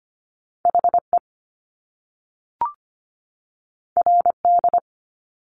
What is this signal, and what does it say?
Morse code, heard as a keyed tone, about 25 wpm, 700 Hz. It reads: HE RD
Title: Top Two Letter Combinations in Words 25wpm